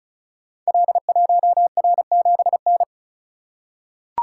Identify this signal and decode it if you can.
Morse code, heard as a keyed tone, about 36 wpm, 700 Hz. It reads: L1R7N
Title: Code Groups - 5 Letter:Numbers - 35wpm